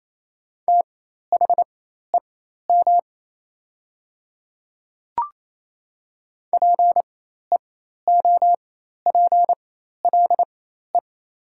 Morse code, heard as a keyed tone, about 28 wpm, spaced as for 13 wpm, 700 Hz. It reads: THEM PEOPLE
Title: 4x Character Spacing - Top 300 Words - 28wpm